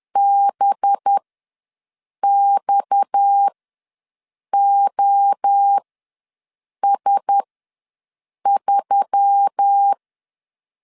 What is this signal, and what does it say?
Morse code, heard as a keyed tone, about 10 wpm, spaced as for 6 wpm, 800 Hz. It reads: BXOS3
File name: LEZIONI / 6 PKX0 / 6 PKX ws 10wpm 3706